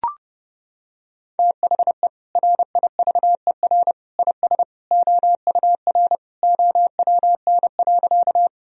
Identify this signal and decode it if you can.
Morse code, heard as a keyed tone, about 32 wpm, 700 Hz. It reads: THE RIVER IS OUR OWN.
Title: Sentences from Top 1000 Words - Review Part 1 - 30wpm